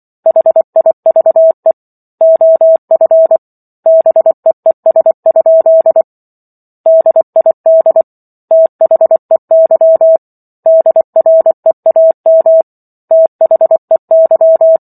Morse code, heard as a keyed tone, about 24 wpm, 650 Hz. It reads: HIVE OF BEES? DID THEY DREAM THEY